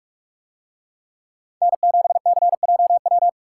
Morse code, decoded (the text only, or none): N7CJW